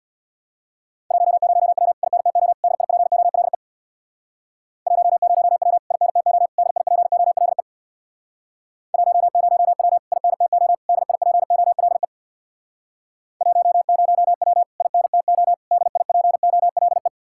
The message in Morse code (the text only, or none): 10W INTO DIPOLE 10W INTO DIPOLE 10W INTO DIPOLE 10W INTO DIPOLE